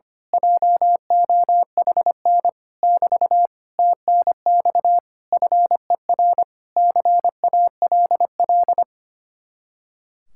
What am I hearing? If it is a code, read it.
Morse code, heard as a keyed tone, about 24 wpm, 700 Hz. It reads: JOHN <BT> TNX FER CALL
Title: Speed-Racing Common QSO Phrases Encore - 25wpm